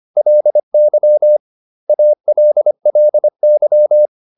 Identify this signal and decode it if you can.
Morse code, heard as a keyed tone, about 25 wpm, 600 Hz. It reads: LY ALLY